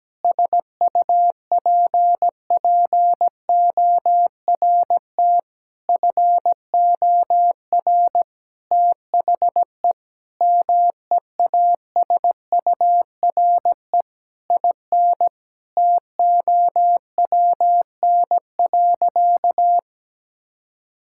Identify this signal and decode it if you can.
Morse code, heard as a keyed tone, about 17 wpm, 700 Hz. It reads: SUPPORT FOR THE MEASURE IN TOWN.
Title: Sentences from Top 1000 Words - Review Part 5 - 17wpm